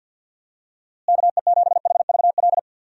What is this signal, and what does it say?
Morse code, heard as a keyed tone, about 47 wpm, 700 Hz. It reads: KE7HVL
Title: USA Call Signs - 6 Characters - Repeated 2x - 50wpm